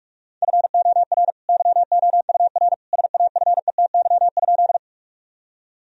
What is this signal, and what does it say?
Morse code, heard as a keyed tone, about 46 wpm, 700 Hz. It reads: FOR YOUR SAFETY?
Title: Sentences from Top 1000 Words - Review Part 4 - 45wpm